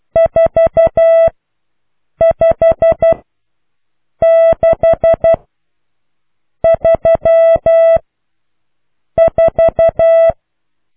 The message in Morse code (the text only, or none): 45634